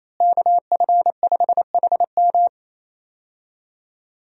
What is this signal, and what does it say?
Morse code, heard as a keyed tone, about 28 wpm, 700 Hz. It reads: KF5HM
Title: USA Call Signs - 5 Characters - Repeated 3x - 28wpm